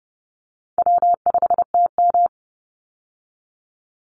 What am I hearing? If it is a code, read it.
Morse code, heard as a keyed tone, about 30 wpm, 700 Hz. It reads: W5TM
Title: Call Signs - 30wpm